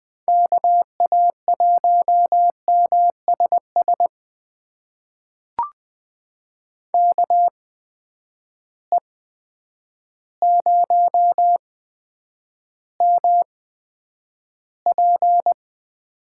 Morse code, heard as a keyed tone, about 20 wpm, spaced as for 9 wpm, 700 Hz. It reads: KA1MSS KE0MP
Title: USA Call Signs - 6 Characters - 8x Character Spacing - 20wpm